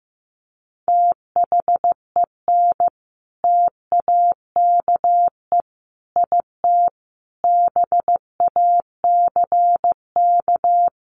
Morse code, heard as a keyed tone, about 15 wpm, 700 Hz. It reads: THEN TAKE IT BACK
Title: Sentences from Top 1500 Words 15wpm